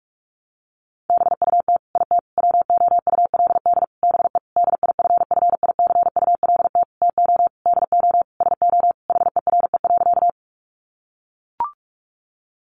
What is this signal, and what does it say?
Morse code, heard as a keyed tone, about 44 wpm, 700 Hz. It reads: BUT IT WOULD BE DIFFICULT TO DO SO HERE.
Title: Sentences from Top 600 Words 45wpm